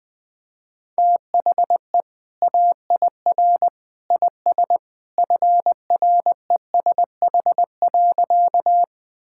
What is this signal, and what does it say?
Morse code, heard as a keyed tone, about 20 wpm, 700 Hz. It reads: THE AIR IS FRESH.
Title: Sentences with 2-Word Phrases - 20wpm